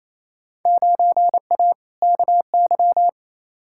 Morse code, heard as a keyed tone, about 28 wpm, 700 Hz. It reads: 9A KY
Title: Field Day Exchanges Encore - 28wpm